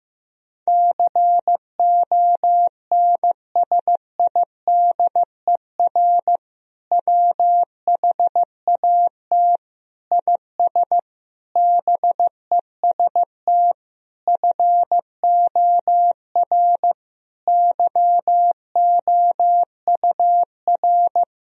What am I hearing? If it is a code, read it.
Morse code, heard as a keyed tone, about 15 wpm, 700 Hz. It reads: CONSIDER WHAT IS BEST FOR YOUR
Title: Sentences from Top 1000 Words - Review Part 4 - 15wpm